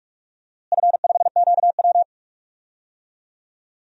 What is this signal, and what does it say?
Morse code, heard as a keyed tone, about 43 wpm, 690 Hz. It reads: F5QW
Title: DXCC-100 Call Signs - 4 Characters - Repeated 3x - 45wpm